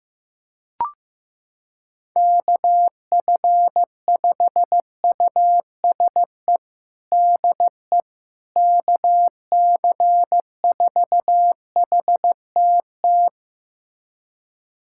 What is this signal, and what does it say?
Morse code, heard as a keyed tone, about 15 wpm, 700 Hz. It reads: KF5USE DE KC4HTT
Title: USA Callsign DE Callsign 15wpm